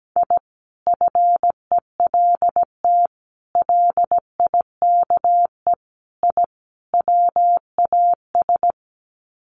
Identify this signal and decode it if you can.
Morse code, heard as a keyed tone, about 17 wpm, 700 Hz. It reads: I FELT LIKE I WAS
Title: Sets of 5 Words Encore 17wpm